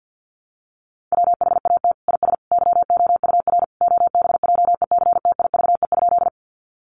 Morse code, heard as a keyed tone, about 50 wpm, 700 Hz. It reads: WHAT IS YOUR OBJECTIVE?
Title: Sentences from Top 1200 Words 50wpm